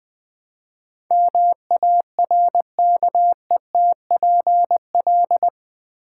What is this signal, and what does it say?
Morse code, heard as a keyed tone, about 20 wpm, 700 Hz. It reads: MARKETPL
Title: Build A Word - 4 Letters At A Time - 11-Letter Words - 20wpm